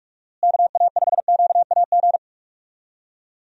Morse code, heard as a keyed tone, about 46 wpm, 700 Hz. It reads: KA5QAG
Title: USA Call Signs - 6 Characters - Repeated 3x - 45wpm